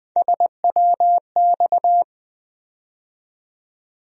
Morse code, heard as a keyed tone, about 20 wpm, 700 Hz. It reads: SWX
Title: USA Call Sign Postfixes - 3 Letters 20wpm